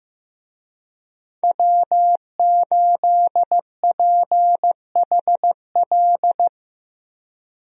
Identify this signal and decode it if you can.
Morse code, heard as a keyed tone, about 15 wpm, 700 Hz. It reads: W8PHL
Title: USA Call Signs - 5 Characters - 15wpm